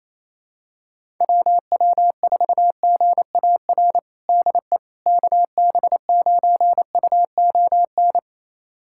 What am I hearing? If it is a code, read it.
Morse code, heard as a keyed tone, about 28 wpm, 700 Hz. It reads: WW4GAR DE KB9UON
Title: USA Callsign DE Callsign - 6 Characters - 28wpm